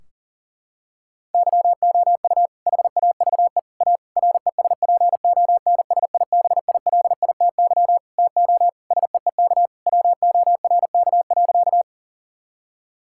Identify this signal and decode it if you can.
Morse code, heard as a keyed tone, about 40 wpm, 700 Hz. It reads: YOU HAVE A RESPONSIBILITY TO SEEK WORK.
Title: Sentences from Top 1600 Words 40wpm